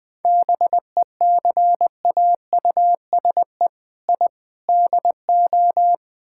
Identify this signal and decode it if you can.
Morse code, heard as a keyed tone, about 20 wpm, 700 Hz. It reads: BECAUSE I DO
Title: Build A Phrase - 4-Word Phrases - 20wpm